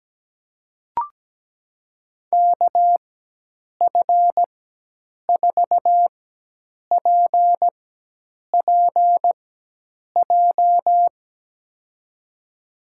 Morse code, heard as a keyed tone, about 17 wpm, spaced as for 8 wpm, 700 Hz. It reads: KF4PPJ
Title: USA Call Signs - 4x Character Spacing - 17wpm